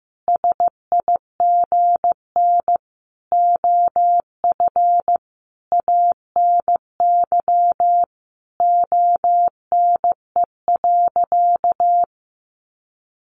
Morse code, heard as a keyed tone, about 15 wpm, 700 Hz. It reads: SIGN OF ANY ONE.
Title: Sentences from Top 1000 Words 15wpm